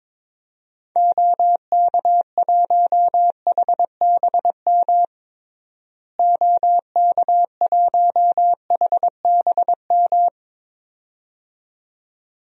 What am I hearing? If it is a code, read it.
Morse code, heard as a keyed tone, about 22 wpm, 700 Hz. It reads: OK1HBM OK1HBM
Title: DXCC-100 Call Signs - 6 Characters - Repeated 2x - 22wpm